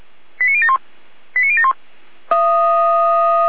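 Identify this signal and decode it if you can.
Five-tone selective call (ZVEI), heard as quick runs of five tones, tones of about 80 ms. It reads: 89071 89071
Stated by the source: ZVEI five-tone selective call 89033 received on SDR.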